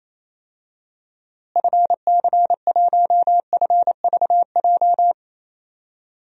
Morse code, heard as a keyed tone, about 28 wpm, 700 Hz. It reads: FC1FVJ
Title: DXCC-100 Call Signs - 6 Characters - Repeated 2x - 28wpm